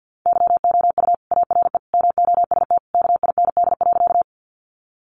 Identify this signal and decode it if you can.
Morse code, heard as a keyed tone, about 50 wpm, 700 Hz. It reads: YOU ARE MOST KIND.
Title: Sentences from Top 200 Words 50wpm